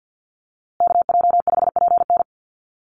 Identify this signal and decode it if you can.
Morse code, heard as a keyed tone, about 50 wpm, 700 Hz. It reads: KJ5PN
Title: USA Call Signs Repeated 2x 50wpm